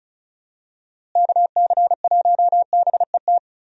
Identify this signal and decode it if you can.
Morse code, heard as a keyed tone, about 34 wpm, 700 Hz. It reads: KC1BET
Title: USA Call Signs - 6 Characters - Repeated 2x - 35wpm